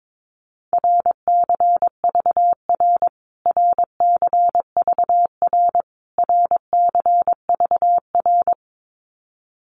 Morse code, heard as a keyed tone, about 22 wpm, 700 Hz. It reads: RCVR RCVR RCVR
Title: Abbreviations Repeated 3x - 22wpm